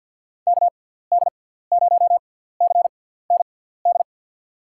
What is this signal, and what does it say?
Morse code, heard as a keyed tone, about 47 wpm, spaced as for 17 wpm, 700 Hz. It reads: KD0CND
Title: USA Call Signs - 6 Characters - 6x Character Spacing - 50wpm